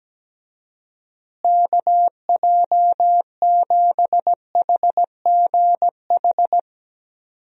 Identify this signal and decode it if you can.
Morse code, heard as a keyed tone, about 17 wpm, 700 Hz. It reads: KJ7HGH
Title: USA Call Signs - 6 Characters - Repeated 2x - 17wpm